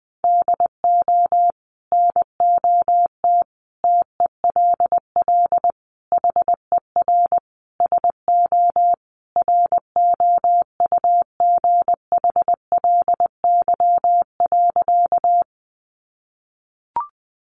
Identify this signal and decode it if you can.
Morse code, heard as a keyed tone, about 20 wpm, 700 Hz. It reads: DO NOT TELL HER SO ROUGHLY.
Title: Sentences from Top 1500 Words 20wpm